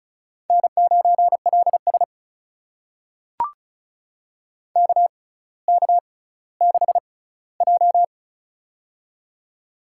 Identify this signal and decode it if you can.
Morse code, heard as a keyed tone, about 36 wpm, spaced as for 18 wpm, 700 Hz. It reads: N9LS KK6J
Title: USA Call Signs - 4 Characters - 6x Character Spacing - 35wpm